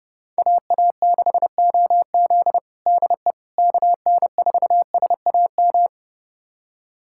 Morse code, heard as a keyed tone, about 30 wpm, 700 Hz. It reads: AA6OZ DE KN4SAM
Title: USA Callsign DE Callsign 30wpm